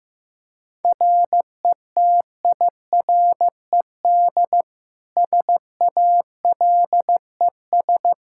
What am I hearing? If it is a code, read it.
Morse code, heard as a keyed tone, about 15 wpm, 700 Hz. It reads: RETIRED SALES
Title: Practice QSO - Exchange 3 - More Info - 15wpm